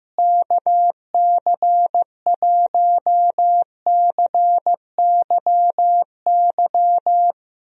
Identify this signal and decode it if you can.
Morse code, heard as a keyed tone, about 15 wpm, 700 Hz. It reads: KC1CYY
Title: USA Call Signs - 6 Characters - Repeated 2x - 15wpm